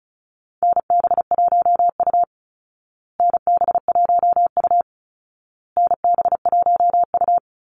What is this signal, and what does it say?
Morse code, heard as a keyed tone, about 34 wpm, 700 Hz. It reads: NB1U NB1U NB1U
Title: Call Signs Repeated 2x - 35wpm